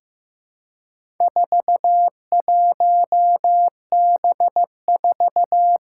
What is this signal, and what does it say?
Morse code, heard as a keyed tone, about 15 wpm, 700 Hz. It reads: 41B4
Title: Four Challenging Letter-Number Combinations 15wpm